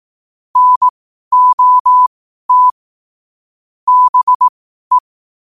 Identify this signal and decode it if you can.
Morse code, heard as a keyed tone, about 18 wpm, spaced as for 12 wpm, 1000 Hz. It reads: NOT BE